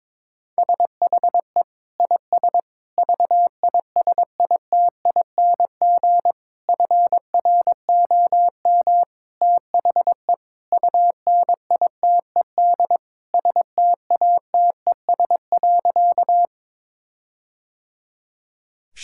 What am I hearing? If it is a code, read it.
Morse code, heard as a keyed tone, about 22 wpm, 700 Hz. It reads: SHE IS VISITING FROM THE UNITED STATES.
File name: practice / 22wpm / Sentences with 4-Word Phrases - 22wpm